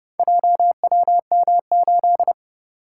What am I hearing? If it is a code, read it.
Morse code, heard as a keyed tone, about 30 wpm, 700 Hz. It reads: JWM8